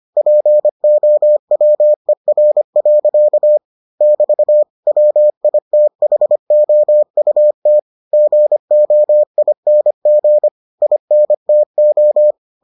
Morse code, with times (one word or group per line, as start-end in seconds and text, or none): 0.2-3.6 POWER.
4.0-4.6 <BT>
4.9-7.8 WITHOUT
8.1-10.5 GOING
10.8-12.3 INTO